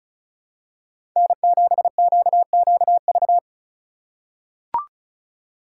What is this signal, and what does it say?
Morse code, heard as a keyed tone, about 34 wpm, 700 Hz. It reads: N7QQV
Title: USA Call Signs - 4x Character Spacing - 35wpm